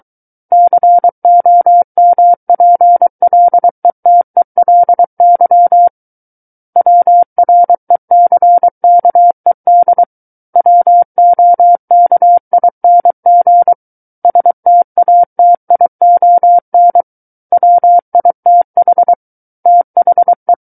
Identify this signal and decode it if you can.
Morse code, heard as a keyed tone, about 23 wpm, 700 Hz. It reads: COMPLETELY WRECKED WOKING STATION WITH THE
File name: Book One / 14-Chapter 14 - 23 WPM